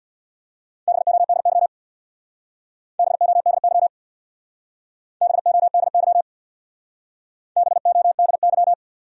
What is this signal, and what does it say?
Morse code, heard as a keyed tone, about 66 wpm, 700 Hz. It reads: BODY BODY BODY BODY